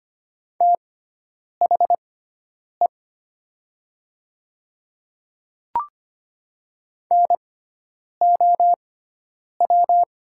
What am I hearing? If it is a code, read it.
Morse code, heard as a keyed tone, about 25 wpm, spaced as for 9 wpm, 700 Hz. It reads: THE NOW
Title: 6x Character Spacing - Top 100 Words - 25wpm